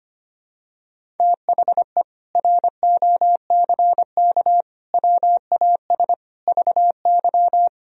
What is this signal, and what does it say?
Morse code, heard as a keyed tone, about 25 wpm, 700 Hz. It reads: THE ROCK WAS VY
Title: Sentences from Top 400 Words and Abbreviations 25wpm